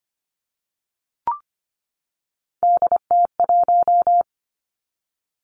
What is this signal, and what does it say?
Morse code, heard as a keyed tone, about 25 wpm, 700 Hz. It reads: DT1
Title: Three Challenging Letter-Number Combinations 25wpm